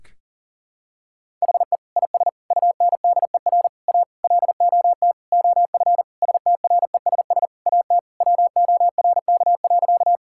Morse code, heard as a keyed tone, about 40 wpm, 700 Hz. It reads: HE IS UNDER A LOT OF STRESS AT WORK.
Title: Sentences from Top 1000 Words - Review Part 6 - 40wpm